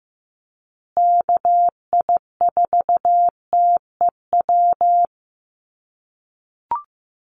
Morse code, heard as a keyed tone, about 15 wpm, 700 Hz. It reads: KI4TEW
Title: USA Call Signs Repeated 3x 15wpm